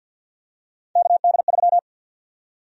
Morse code, heard as a keyed tone, about 52 wpm, 700 Hz. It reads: KD3
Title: USA Call Sign Prefixes - 50wpm